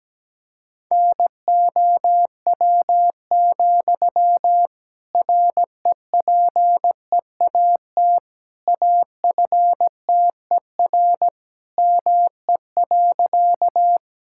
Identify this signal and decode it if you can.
Morse code, heard as a keyed tone, about 17 wpm, 700 Hz. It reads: NOW, REPEAT AFTER ME.
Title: Sentences from Top 1600 Words 17wpm